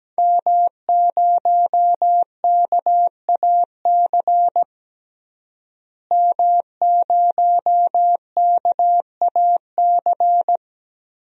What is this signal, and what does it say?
Morse code, heard as a keyed tone, about 17 wpm, 700 Hz. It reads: M0KAC M0KAC
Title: DXCC-100 Call Signs - 5 Characters - Repeated 3x - 17wpm